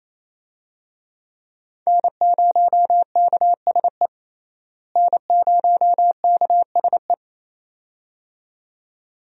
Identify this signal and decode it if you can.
Morse code, heard as a keyed tone, about 28 wpm, 700 Hz. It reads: N0KSE N0KSE
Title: USA Call Signs - 5 Characters - Repeated 2x - 28wpm